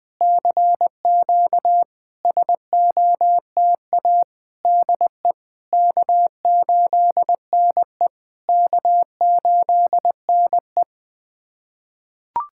Morse code, heard as a keyed tone, about 20 wpm, 660 Hz. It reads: CQ SOTA DE K8NE K8NE
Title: SOTA QSO Practice 20wpm